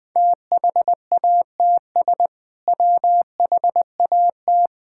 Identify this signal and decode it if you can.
Morse code, heard as a keyed tone, about 20 wpm, 700 Hz. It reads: THATS WHAT
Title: Sentences with 2-Word Phrases - Spoken Beginning - 20wpm